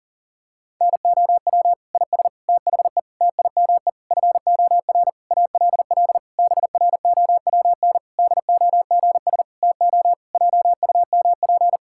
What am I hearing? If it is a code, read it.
Morse code, heard as a keyed tone, about 40 wpm, 700 Hz. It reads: NOW IS THE TIME FOR ALL BROWN DOGS TO JUMP
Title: Pangram Challenge v2 - 40wpm